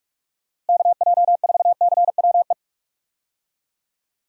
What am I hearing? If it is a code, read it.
Morse code, heard as a keyed tone, about 44 wpm, 700 Hz. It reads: KJ4CWE
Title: USA Call Signs - 6 Characters - Repeated 2x - 45wpm